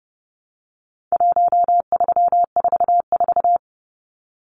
Morse code, heard as a keyed tone, about 30 wpm, 700 Hz. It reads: 1344
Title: Numbers - 4-Digit 30wpm